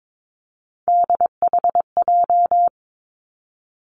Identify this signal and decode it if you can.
Morse code, heard as a keyed tone, about 22 wpm, 700 Hz. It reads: DHJ
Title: Three Challenging Letter-Number Combinations 22wpm